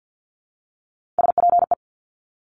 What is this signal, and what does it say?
Morse code, heard as a keyed tone, about 50 wpm, 700 Hz. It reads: SPE
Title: Top Three Letter Combinations in Words 50wpm